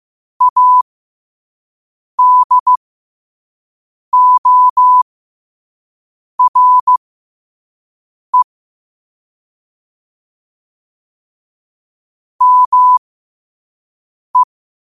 Morse code, heard as a keyed tone, about 15 wpm, spaced as for 5 wpm, 1000 Hz. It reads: ADORE ME
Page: Quotes for Fri, 15 Aug 2025 in Morse Code at 5 words per minute.